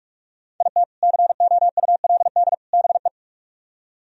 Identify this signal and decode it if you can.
Morse code, heard as a keyed tone, about 44 wpm, 700 Hz. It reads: IT COULD BE